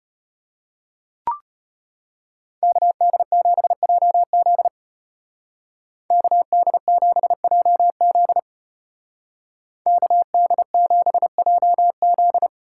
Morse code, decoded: KD7JZ KD7JZ KD7JZ

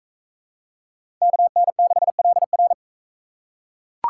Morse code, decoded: KN6LR

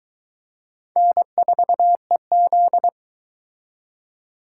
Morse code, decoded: N4EZ

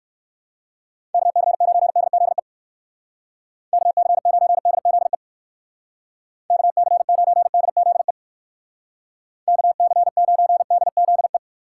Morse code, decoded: KC9DZE KC9DZE KC9DZE KC9DZE

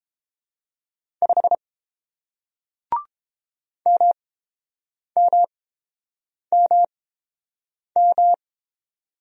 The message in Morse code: 5 MMMM

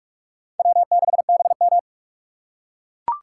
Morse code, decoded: W6BM

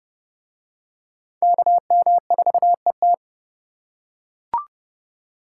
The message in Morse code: KM4ET

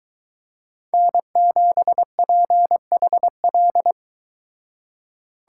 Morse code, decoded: N7PHL

